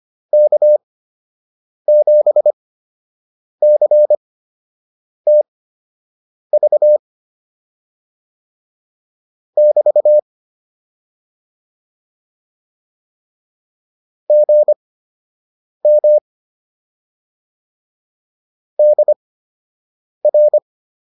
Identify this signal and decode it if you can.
Morse code, heard as a keyed tone, about 25 wpm, spaced as for 7 wpm, 600 Hz. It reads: K7CTV <BT> GM DR